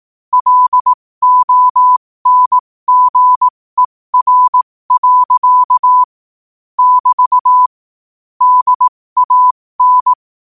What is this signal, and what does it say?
Morse code, heard as a keyed tone, about 18 wpm, spaced as for 15 wpm, 1000 Hz. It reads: LONGER. <BT> DAN
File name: qotdfiles / 15wpm